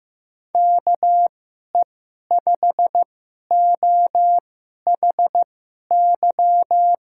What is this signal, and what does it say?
Morse code, heard as a keyed tone, about 15 wpm, spaced as for 11 wpm, 700 Hz. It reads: KE5OHY
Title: USA Call Signs - 6 Characters - 2x Character Spacing - 15wpm